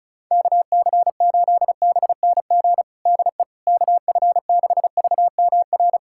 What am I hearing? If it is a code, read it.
Morse code, heard as a keyed tone, about 36 wpm, 700 Hz. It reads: KC8BNG DE KF6VMR
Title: USA Callsign DE Callsign - 6 Characters - 35wpm